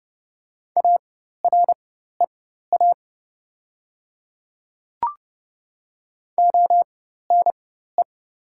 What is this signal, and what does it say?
Morse code, heard as a keyed tone, about 30 wpm, spaced as for 14 wpm, 700 Hz. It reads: AREA ONE
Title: 4x Character Spacing - Top 500 Words - 30wpm